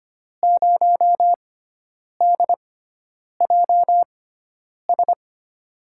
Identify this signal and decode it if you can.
Morse code, heard as a keyed tone, about 25 wpm, spaced as for 9 wpm, 700 Hz. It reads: 0DJS
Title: USA Call Signs - 6 Characters - 6x Character Spacing - 25wpm